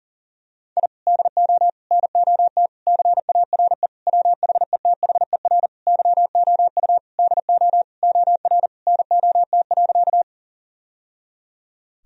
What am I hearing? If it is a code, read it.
Morse code, heard as a keyed tone, about 40 wpm, 700 Hz. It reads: I DO NOT CARE WHETHER YOU DO OR NOT.
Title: Sentences from Top 1000 Words - Review Part 3 - 40wpm